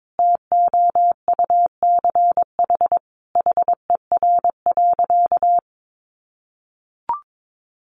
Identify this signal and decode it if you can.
Morse code, heard as a keyed tone, about 22 wpm, 700 Hz. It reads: TOUCH HER.
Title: Sentences from Top 1200 Words 22wpm